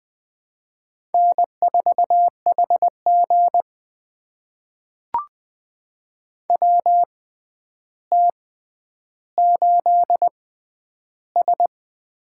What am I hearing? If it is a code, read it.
Morse code, heard as a keyed tone, about 20 wpm, spaced as for 10 wpm, 700 Hz. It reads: N4HG WT8S
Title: USA Call Signs - 4 Characters - 6x Character Spacing - 20wpm